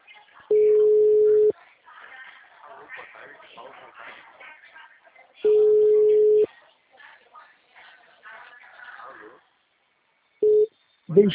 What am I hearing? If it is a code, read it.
Ringback tone (Europe, 425 Hz), with speech.
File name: out- -2009- - - .287